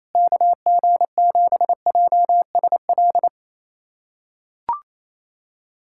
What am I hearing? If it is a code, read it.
Morse code, heard as a keyed tone, about 27 wpm, 700 Hz. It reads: KG7JSL
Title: USA Callsign DE Callsign - 6 Characters - 28wpm